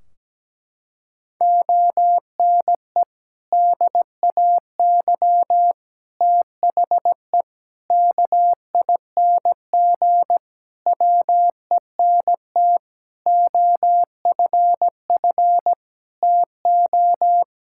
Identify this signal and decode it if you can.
Morse code, heard as a keyed tone, about 17 wpm, 700 Hz. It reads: ONE DAY THE KING WENT OFF TO
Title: Sentences from Top 1600 Words 17wpm